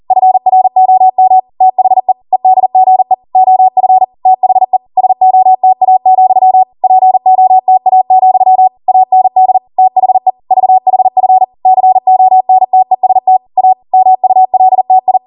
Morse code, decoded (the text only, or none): FROM THE EDGE OF THE SOTA, POTA, AND THE VHF CONTEST A MULTI